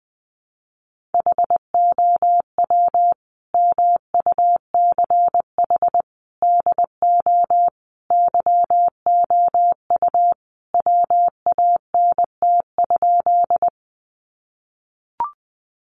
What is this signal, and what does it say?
Morse code, heard as a keyed tone, about 20 wpm, 700 Hz. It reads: HOW MUCH DO YOU WANT?
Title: Sentences from Top 200 Words 20wpm